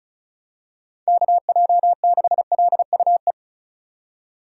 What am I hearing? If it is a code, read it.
Morse code, heard as a keyed tone, about 34 wpm, 700 Hz. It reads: KJ6LUE
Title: USA Call Signs - 2x Character Spacing - 35wpm